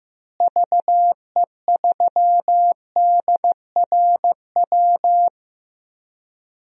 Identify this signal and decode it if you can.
Morse code, heard as a keyed tone, about 15 wpm, 700 Hz. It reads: VE3DRW
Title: DXCC-100 Call Signs - 6 Characters - Repeated 3x - 15wpm